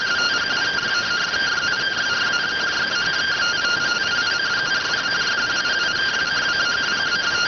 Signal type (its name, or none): two-tone teleprinter signal